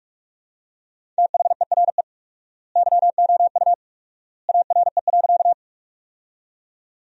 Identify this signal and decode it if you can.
Morse code, heard as a keyed tone, about 43 wpm, 700 Hz. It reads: THERE YOU ARE.
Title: 4x Word-Spacing - Sentences from Top 100 Words - 45wpm